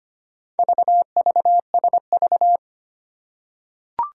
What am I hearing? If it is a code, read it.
Morse code, heard as a keyed tone, about 25 wpm, 700 Hz. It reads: VVSV
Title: Four Challenging Letter-Number Combinations 25wpm